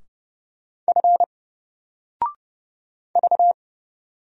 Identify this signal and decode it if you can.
Morse code, heard as a keyed tone, about 30 wpm, 700 Hz. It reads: F V
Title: Single Letter-Number 30wpm